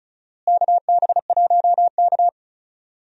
Morse code, heard as a keyed tone, about 34 wpm, 700 Hz. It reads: KB1K
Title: USA Call Signs - 4 Characters - Repeated 2x - 35wpm